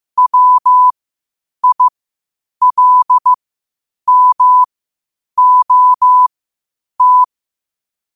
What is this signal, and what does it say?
Morse code, heard as a keyed tone, about 15 wpm, spaced as for 8 wpm, 1000 Hz. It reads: WILMOT
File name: 8wpm